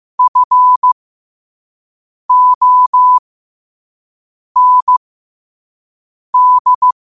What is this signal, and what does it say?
Morse code, heard as a keyed tone, about 15 wpm, spaced as for 5 wpm, 1000 Hz. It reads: FOND
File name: qotdfiles / 5wpm